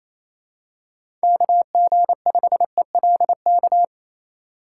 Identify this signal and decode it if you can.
Morse code, heard as a keyed tone, about 28 wpm, 700 Hz. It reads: KG5ELK
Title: USA Call Signs - 6 Characters - 3x Character Spacing - 28wpm